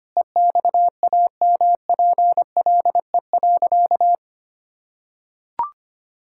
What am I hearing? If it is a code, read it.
Morse code, heard as a keyed tone, about 25 wpm, 700 Hz. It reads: EXAMPLE.
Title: Sentences from Top 1000 Words - Review Part 6 - 25wpm